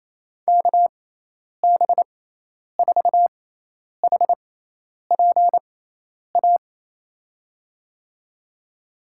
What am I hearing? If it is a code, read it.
Morse code, heard as a keyed tone, about 28 wpm, spaced as for 10 wpm, 700 Hz. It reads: KB4HPA
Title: USA Call Signs - 6x Character Spacing - 28wpm